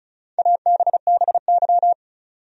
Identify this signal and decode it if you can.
Morse code, heard as a keyed tone, about 36 wpm, 700 Hz. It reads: ABBY
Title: USA Common Names - Max 4 Letters - 35wpm